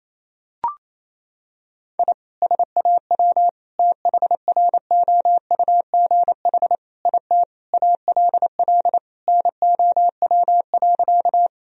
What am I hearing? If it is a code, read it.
Morse code, heard as a keyed tone, about 28 wpm, 700 Hz. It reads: I SAW THROUGH IT ALL NOW.
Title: Sentences from Top 1000 Words - Review Part 1 - 28wpm